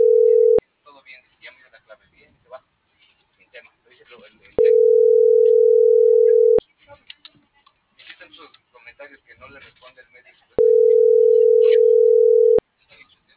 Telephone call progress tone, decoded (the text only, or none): ringback tone